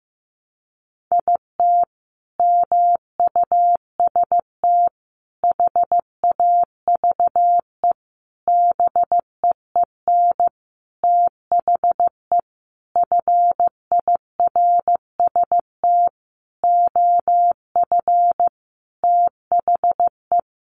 Morse code, heard as a keyed tone, about 15 wpm, 700 Hz. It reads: IT MUST HAVE BEEN THE FIRST OF THE
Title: Sentences from Top 800 Words 15wpm